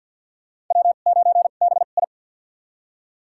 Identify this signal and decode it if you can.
Morse code, heard as a keyed tone, about 46 wpm, spaced as for 35 wpm, 700 Hz. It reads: W9BI
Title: USA Call Signs - 4 Characters - 2x Character Spacing - 50wpm